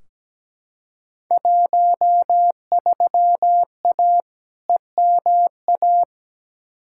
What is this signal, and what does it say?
Morse code, heard as a keyed tone, about 17 wpm, 700 Hz. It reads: 13A EMA
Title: Field Day Exchanges Encore - 17wpm